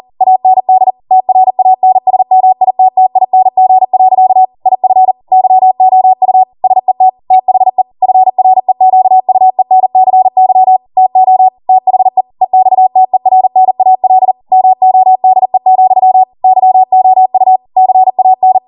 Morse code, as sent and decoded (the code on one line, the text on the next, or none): .- -. -.. / - .-. .- -. ... -- .. - - .. -. --. .-.-.- / .. ..-. / -.-- --- ..- / ... - / - .... / ..-. .-. --.- ..- . -. -.-. -.-- / - --- / - .... / . -..- - . .-. -. .- .-.. / -- --- -.. --..-- / -.-- --- ..- / -.-. .- -.
AND TRANSMITTING. IF YOU SET THE FREQUENCY TO THE EXTERNAL MODE, YOU CAN